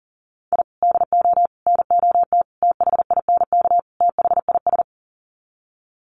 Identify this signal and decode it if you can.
Morse code, heard as a keyed tone, about 40 wpm, 700 Hz. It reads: I DO NOT THINK THIS
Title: Sets of 5 Words Encore 40wpm